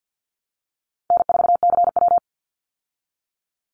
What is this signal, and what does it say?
Morse code, heard as a keyed tone, about 50 wpm, 700 Hz. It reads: N4CW
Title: Easy Call Signs Repeated 2x - 50wpm